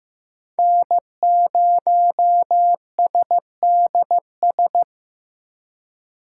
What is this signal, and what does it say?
Morse code, heard as a keyed tone, about 15 wpm, 700 Hz. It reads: N0SDS
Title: USA Call Signs - 5 Characters - 3x Character Spacing - 15wpm